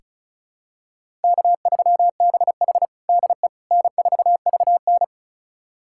Morse code, heard as a keyed tone, about 36 wpm, 700 Hz. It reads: K3BH DE N4VN
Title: USA Callsign DE Callsign - 4 Characters - 35wpm